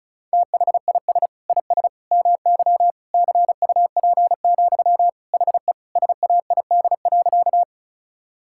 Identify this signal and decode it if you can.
Morse code, heard as a keyed tone, about 36 wpm, 700 Hz. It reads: THIS IS MY CUP, HE SAID.
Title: Sentences from Top 1600 Words 35wpm